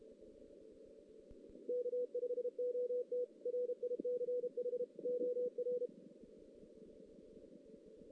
Morse code, heard as a keyed tone, about 33 wpm, 490 Hz. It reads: K5OT RICH OR
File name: 2016 JAN CW NAQP